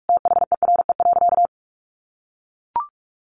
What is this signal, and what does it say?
Morse code, heard as a keyed tone, about 44 wpm, 700 Hz. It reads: THERE.
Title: Sentences from Top 1500 Words 45wpm